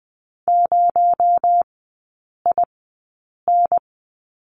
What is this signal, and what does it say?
Morse code, heard as a keyed tone, about 20 wpm, spaced as for 8 wpm, 700 Hz. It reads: 0IN
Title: 218 Lesson 35e - Number 0 - Three Characters 20wpm